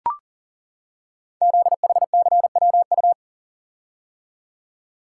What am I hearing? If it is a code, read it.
Morse code, heard as a keyed tone, about 40 wpm, 700 Hz. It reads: ZHCWU
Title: Code Groups - 5 Letters - 40wpm